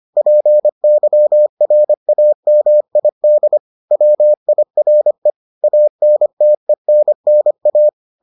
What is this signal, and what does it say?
Morse code, heard as a keyed tone, about 25 wpm, 600 Hz. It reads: PYRAMID WIRE ANTENNA